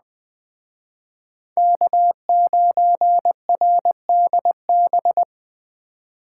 Morse code, heard as a keyed tone, about 20 wpm, 700 Hz. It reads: K9RDB
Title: Speed Racing - USA Call Signs 17wpm